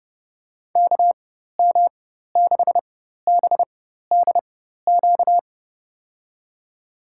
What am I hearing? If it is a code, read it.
Morse code, heard as a keyed tone, about 30 wpm, spaced as for 14 wpm, 700 Hz. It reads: KM6BDQ
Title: USA Call Signs - 6 Characters - 4x Character Spacing - 30wpm